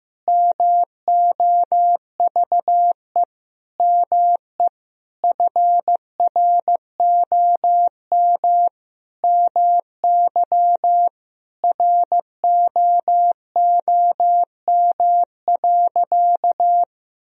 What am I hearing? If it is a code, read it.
Morse code, heard as a keyed tone, about 15 wpm, 700 Hz. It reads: MOVE ME FROM MY ROOM.
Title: Sentences from Top 1000 Words - Review Part 1 - 15wpm